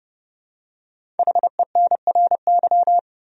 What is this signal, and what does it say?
Morse code, heard as a keyed tone, about 31 wpm, 700 Hz. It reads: HENRY
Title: USA Common Names - Max 5 Letters - 30wpm